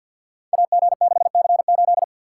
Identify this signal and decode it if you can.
Morse code, heard as a keyed tone, about 48 wpm, 700 Hz. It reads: AG6C8
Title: Code Groups - 5 Letter:Numbers - 50wpm